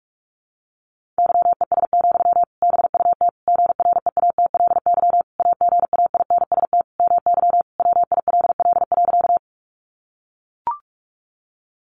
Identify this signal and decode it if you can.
Morse code, heard as a keyed tone, about 44 wpm, 700 Hz. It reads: YES, BUT GREATLY AGAINST MY WILL.
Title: Sentences from Top 1400 Words 45wpm